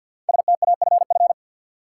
Morse code, heard as a keyed tone, about 52 wpm, 700 Hz. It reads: STAFF
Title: Sets of 5 Words Encore - New Word Drill - 50wpm